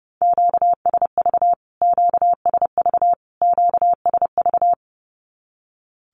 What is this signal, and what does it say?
Morse code, heard as a keyed tone, about 30 wpm, 700 Hz. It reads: QSV QSV QSV
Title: Q-Codes Repeated 3x 30wpm